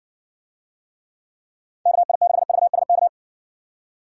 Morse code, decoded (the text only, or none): KI6VSK